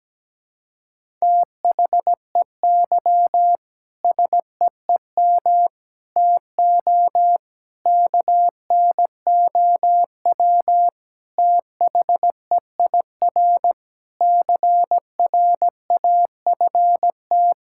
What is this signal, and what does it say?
Morse code, heard as a keyed tone, about 17 wpm, 700 Hz. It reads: THEY SEEM TO KNOW THEIR CRAFT
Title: Sentences from Top 1000 Words - Review Part 6 - 17wpm